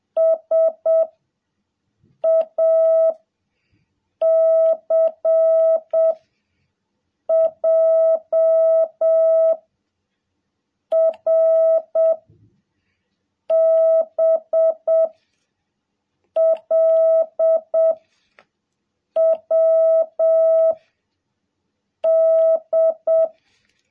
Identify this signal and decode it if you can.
Morse code, heard as a keyed tone, about 7 wpm, 660 Hz. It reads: S A C J R B L W D
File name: jh8cbh / 2021cw